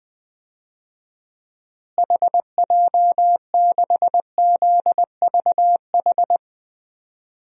Morse code, heard as a keyed tone, about 20 wpm, 700 Hz. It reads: HJ6ZVH
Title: DXCC-100 Call Signs - 6 Characters - Repeated 3x - 20wpm